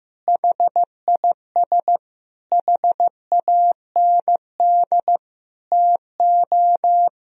Morse code, heard as a keyed tone, about 15 wpm, 700 Hz. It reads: HIS HAND TO
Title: Sentences from Top 1000 Words - Review Part 4 - 15wpm